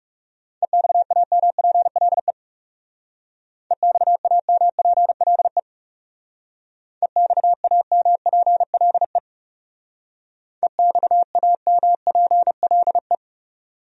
Morse code, heard as a keyed tone, about 37 wpm, 700 Hz. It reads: EXAMPLE EXAMPLE EXAMPLE EXAMPLE
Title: Speed-Racing Top 300 Words 30wpm